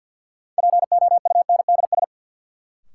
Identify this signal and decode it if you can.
Morse code, heard as a keyed tone, about 48 wpm, 700 Hz. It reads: POUNDS
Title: Sentences from Top 1000 Words - Review Part 2 - New Word Drill - 50wpm